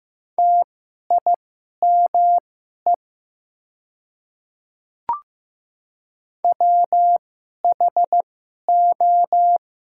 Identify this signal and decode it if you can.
Morse code, heard as a keyed tone, about 15 wpm, spaced as for 11 wpm, 700 Hz. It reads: TIME WHO
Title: 2x Character Spacing - Top 100 Words - 15wpm